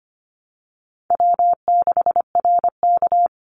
Morse code, heard as a keyed tone, about 25 wpm, 700 Hz. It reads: W6RK
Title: Easy Call Signs Repeated 2x - 25wpm